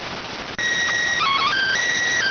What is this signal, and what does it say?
SSTV (slow-scan television), PD 290.